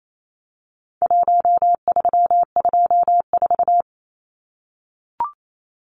Morse code, heard as a keyed tone, about 28 wpm, 700 Hz. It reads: 1324
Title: Numbers - Contest Style - 4-Digit 28wpm